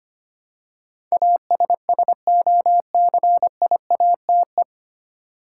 Morse code, heard as a keyed tone, about 25 wpm, 700 Hz. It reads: ASSOCIATE
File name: Nine Letter Words - 25wpm